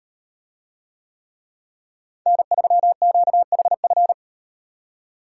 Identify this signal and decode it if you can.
Morse code, heard as a keyed tone, about 38 wpm, 700 Hz. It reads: N3QHF